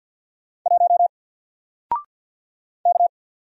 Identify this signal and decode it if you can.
Morse code, heard as a keyed tone, about 50 wpm, 700 Hz. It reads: 1 K
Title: Single Letter-Number 50wpm